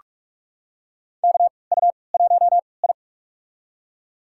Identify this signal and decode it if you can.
Morse code, heard as a keyed tone, about 46 wpm, spaced as for 26 wpm, 700 Hz. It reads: KU1I
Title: USA Call Signs - 4 Characters - 3x Character Spacing - 45wpm